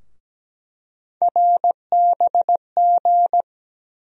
Morse code, heard as a keyed tone, about 17 wpm, 700 Hz. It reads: RBG
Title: USA Call Sign Postfixes - 3 Letters 17wpm